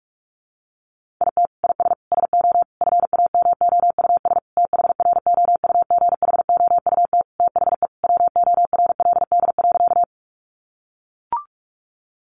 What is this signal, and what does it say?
Morse code, heard as a keyed tone, about 44 wpm, 700 Hz. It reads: IT IS SO FAMOUS THROUGHOUT THE WORLD.
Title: Sentences from Top 1500 Words 45wpm